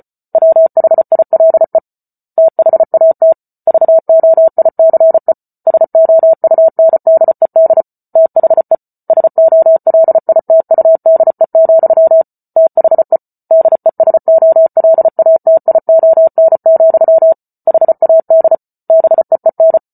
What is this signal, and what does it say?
Morse code, heard as a keyed tone, about 34 wpm, 650 Hz. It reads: WHILE THAT VOICE SOUNDED THE SOLITUDE, THE DESOLATION, HAD BEEN